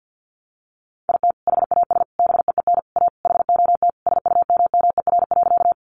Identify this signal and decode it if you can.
Morse code, heard as a keyed tone, about 50 wpm, 700 Hz. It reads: IT HAS BEEN A HOT SUMMER.
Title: Sentences from Top 1100 Words 50wpm